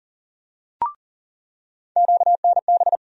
Morse code, heard as a keyed tone, about 40 wpm, 700 Hz. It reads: QNB
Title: USA Call Sign Postfixes - 3 Letters 40wpm